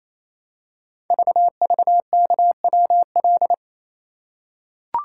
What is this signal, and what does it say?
Morse code, heard as a keyed tone, about 27 wpm, 700 Hz. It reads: VVKWL
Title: Code Groups - 5 Letters - 28wpm